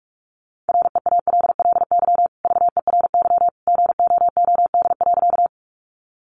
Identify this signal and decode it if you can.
Morse code, heard as a keyed tone, about 44 wpm, 700 Hz. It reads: REALLY VERY GOOD.
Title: Sentences from Top 600 Words 45wpm